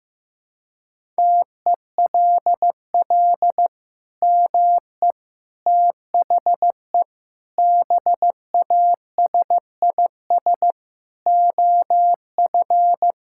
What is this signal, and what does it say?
Morse code, heard as a keyed tone, about 15 wpm, 700 Hz. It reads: TELL ME THE BASIS OF
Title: Sentences from Top 1000 Words - Review Part 5 - 15wpm